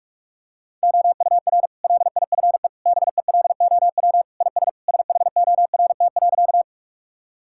Morse code, heard as a keyed tone, about 46 wpm, 700 Hz. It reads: OUR LIFE BELOW IS SHORT.
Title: Sentences from Top 1000 Words - Review Part 5 - 45wpm